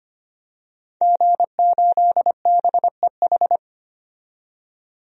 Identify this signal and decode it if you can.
Morse code, heard as a keyed tone, about 25 wpm, 700 Hz. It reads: G8BEH
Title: DXCC-100 Call Signs - 5 Characters - 25wpm